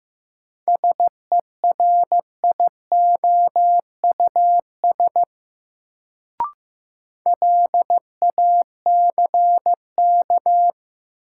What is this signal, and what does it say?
Morse code, heard as a keyed tone, about 15 wpm, 700 Hz. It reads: SERIOUS LACK
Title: Sentences from Top 1100 Words - New Word Drill - 15wpm